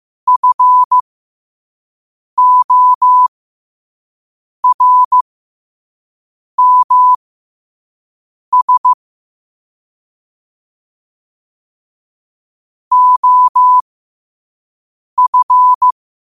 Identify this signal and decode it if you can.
Morse code, heard as a keyed tone, about 15 wpm, spaced as for 5 wpm, 1000 Hz. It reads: FORMS OF
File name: qotdfiles / 5wpm